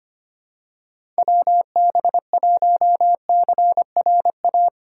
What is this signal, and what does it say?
Morse code, heard as a keyed tone, about 25 wpm, 700 Hz. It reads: WB1CRA
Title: USA Call Signs - 6 Characters - 3x Character Spacing - 25wpm